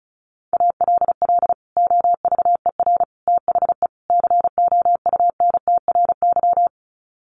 Morse code, heard as a keyed tone, about 34 wpm, 700 Hz. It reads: ALL OVER THE COUNTRY
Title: Sets of 4 Words 35wpm